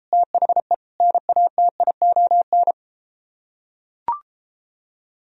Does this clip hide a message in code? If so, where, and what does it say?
Yes: Morse code from 0.1 to 2.7 s, heard as a keyed tone, about 33 wpm, 700 Hz. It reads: THE NATION